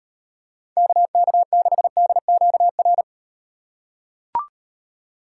Morse code, heard as a keyed tone, about 38 wpm, 700 Hz. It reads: KK6DQR